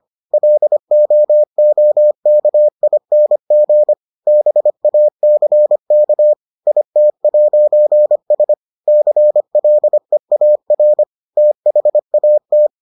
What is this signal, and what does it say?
Morse code, heard as a keyed tone, about 25 wpm, 600 Hz. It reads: LOOKING BACK IT'S CLEAR THAT